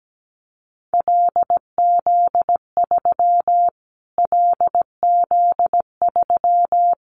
Morse code, heard as a keyed tone, about 17 wpm, 700 Hz. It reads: LZ3 LZ3
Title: Callsign Prefixes for Top 25 DXCC - Repeated 3x - 17wpm